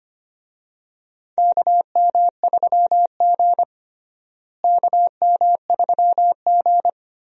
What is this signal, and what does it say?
Morse code, heard as a keyed tone, about 25 wpm, 700 Hz. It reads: KM3G KM3G
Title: USA Call Signs - 4 Characters - Repeated 2x - 25wpm